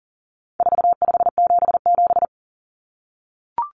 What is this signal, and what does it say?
Morse code, heard as a keyed tone, about 40 wpm, 700 Hz. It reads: 4577
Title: Common Cut Numbers - 4-Digit 40wpm